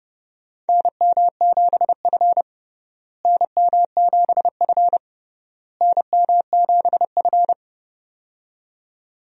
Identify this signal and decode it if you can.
Morse code, heard as a keyed tone, about 32 wpm, 700 Hz. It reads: NM7F NM7F NM7F
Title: USA Call Signs - 4 Characters - Repeated 3x - 30wpm